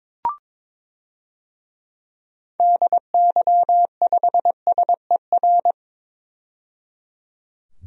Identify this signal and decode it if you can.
Morse code, heard as a keyed tone, about 22 wpm, 700 Hz. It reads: DY5SER
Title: DXCC-100 Call Signs - 6 Characters - 22wpm